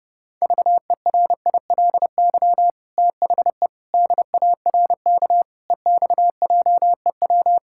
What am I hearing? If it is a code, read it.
Morse code, heard as a keyed tone, about 30 wpm, 700 Hz. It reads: VERILY THE DARK EXJEW
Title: Pangram Challenge v2 - 30wpm